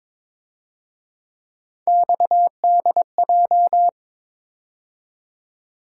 Morse code, heard as a keyed tone, about 26 wpm, 700 Hz. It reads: XDJ
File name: Speed-Racing - Three Letter-Number Combinations - 22wpm